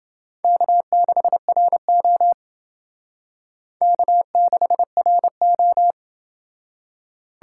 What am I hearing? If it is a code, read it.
Morse code, heard as a keyed tone, about 29 wpm, 700 Hz. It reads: K6RO K6RO